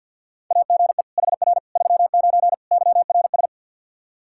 Morse code, heard as a keyed tone, about 49 wpm, 700 Hz. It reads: AGE HR 39 YRS
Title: Practice QSO - Exchange 3 - More Info - 50wpm